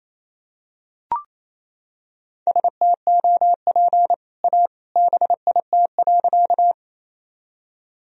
Morse code, heard as a keyed tone, about 28 wpm, 700 Hz. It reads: STOP A BIT.
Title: Sentences from Top 1000 Words - Review Part 1 - 28wpm